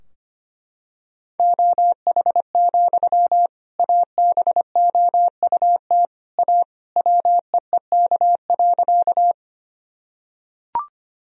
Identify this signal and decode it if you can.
Morse code, heard as a keyed tone, about 25 wpm, 700 Hz. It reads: OH, ABOUT A WEEK.
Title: Sentences from Top 1000 Words - Review Part 2 - 25wpm